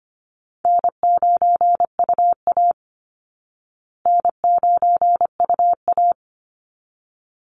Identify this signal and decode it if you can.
Morse code, heard as a keyed tone, about 25 wpm, 700 Hz. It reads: N9UA N9UA
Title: Easy Call Signs Repeated 2x - 25wpm